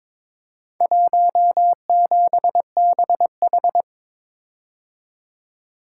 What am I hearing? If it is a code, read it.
Morse code, heard as a keyed tone, about 22 wpm, 690 Hz. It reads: 17BH
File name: practice / 22wpm / Four Letter-Number Combinations - 22wpm